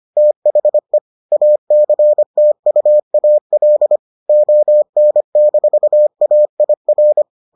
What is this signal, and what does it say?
Morse code, heard as a keyed tone, about 25 wpm, 600 Hz. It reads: THE ACTUAL ON-AIR